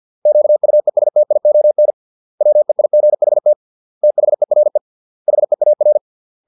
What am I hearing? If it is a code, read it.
Morse code, heard as a keyed tone, about 50 wpm, 600 Hz. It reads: QUESTION WEIGHT THERE HEAR